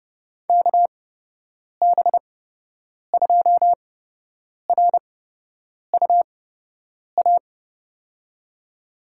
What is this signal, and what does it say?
Morse code, heard as a keyed tone, about 29 wpm, spaced as for 8 wpm, 700 Hz. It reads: KB2RUA
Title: USA Call Signs - 8x Character Spacing - 30wpm